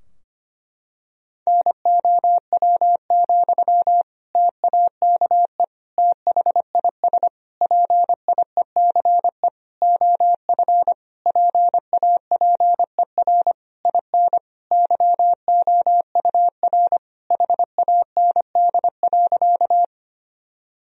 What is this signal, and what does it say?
Morse code, heard as a keyed tone, about 25 wpm, 700 Hz. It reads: NOW, TAKE THIS PIECE OF PAPER IN YOUR HAND.
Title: Sentences from Top 1000 Words - Review Part 2 - 25wpm